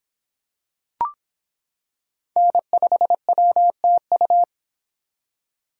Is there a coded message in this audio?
Yes: Morse code from 2.4 to 4.4 s, heard as a keyed tone, about 25 wpm, 700 Hz. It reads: N5WTU